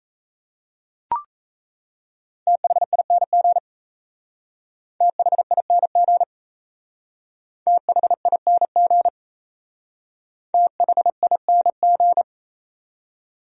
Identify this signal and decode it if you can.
Morse code, heard as a keyed tone, about 33 wpm, 700 Hz. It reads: THING THING THING THING
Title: Speed-Racing Top 100 Words 28wpm